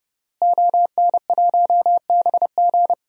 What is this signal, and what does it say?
Morse code, heard as a keyed tone, about 30 wpm, 700 Hz. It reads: ON1BG
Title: DXCC-100 Call Signs - 5 Characters - Repeated 3x - 30wpm